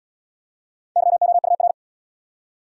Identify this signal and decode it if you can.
Morse code, heard as a keyed tone, about 70 wpm, 700 Hz. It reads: YOUR